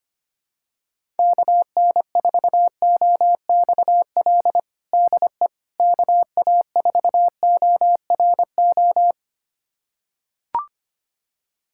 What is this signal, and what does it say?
Morse code, heard as a keyed tone, about 25 wpm, 700 Hz. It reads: KN4OXL DE KA4ORO
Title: USA Callsign DE Callsign 25wpm